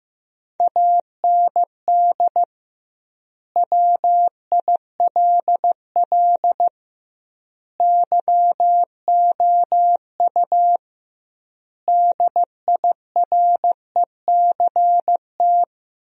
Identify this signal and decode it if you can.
Morse code, heard as a keyed tone, about 15 wpm, 700 Hz. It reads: AND WILL YOU DIRECT
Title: 2x Word-Spacing - Sentences from Top 400 Words - 15wpm